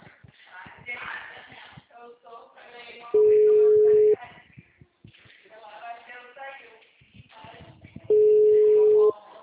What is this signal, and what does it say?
Ringback tone (Europe, 425 Hz), with speech.
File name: out- -2016- - - .11408